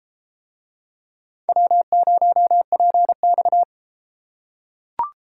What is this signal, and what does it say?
Morse code, heard as a keyed tone, about 33 wpm, 700 Hz. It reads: W0PX